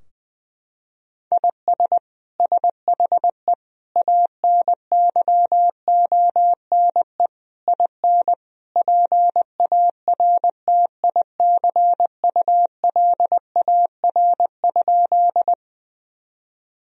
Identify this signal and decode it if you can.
Morse code, heard as a keyed tone, about 21 wpm, 700 Hz. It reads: IS SHE ANYONE IN PARTICULAR?
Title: Sentences from Top 1000 Words - Review Part 4 - 20wpm